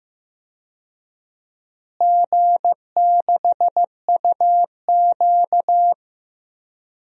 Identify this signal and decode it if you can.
Morse code, heard as a keyed tone, about 15 wpm, 700 Hz. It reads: G6UQ
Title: DXCC-100 Call Signs - 4 Characters - Repeated 2x - 15wpm